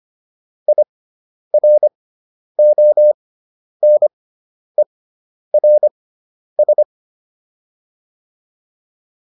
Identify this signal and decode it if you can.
Morse code, heard as a keyed tone, about 25 wpm, spaced as for 10 wpm, 600 Hz. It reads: IRONERS